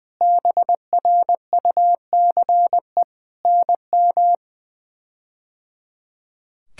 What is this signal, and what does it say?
Morse code, heard as a keyed tone, about 21 wpm, 700 Hz. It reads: BRUCE NM
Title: K1USN SST QSO Practice - 20wpm